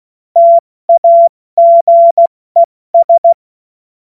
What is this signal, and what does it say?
Morse code, heard as a keyed tone, about 16 wpm, spaced as for 14 wpm, 680 Hz. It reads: TAGES
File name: Morsekurs / Witze / Witz 31. Mai 2022 70 F80